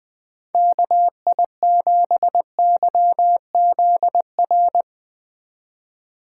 Morse code, heard as a keyed tone, about 20 wpm, 700 Hz. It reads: KI7YZR